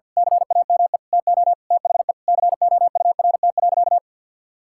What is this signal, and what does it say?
Morse code, heard as a keyed tone, about 48 wpm, 700 Hz. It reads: CAME TO THE COURT.
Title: Sentences from Top 1600 Words 50wpm